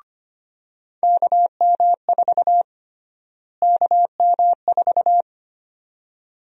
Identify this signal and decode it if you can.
Morse code, heard as a keyed tone, about 25 wpm, 700 Hz. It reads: KM4 KM4
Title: USA Call Sign Prefixes - Repeated 2x - 25wpm